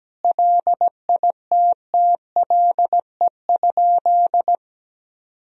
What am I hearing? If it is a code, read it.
Morse code, heard as a keyed tone, about 17 wpm, 700 Hz. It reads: LITTLE?
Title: Sentences from Top 500 Words and Abbreviations 17wpm